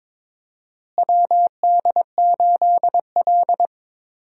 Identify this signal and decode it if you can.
Morse code, heard as a keyed tone, about 22 wpm, 700 Hz. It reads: WD8L
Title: USA Call Signs - 4 Characters - Repeated 2x - 22wpm